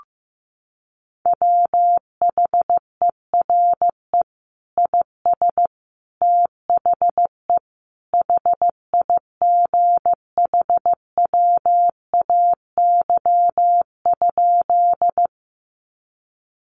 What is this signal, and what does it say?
Morse code, heard as a keyed tone, about 15 wpm, 700 Hz. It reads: WHERE IS THE HIGHWAY?
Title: Sentences from Top 1400 Words 15wpm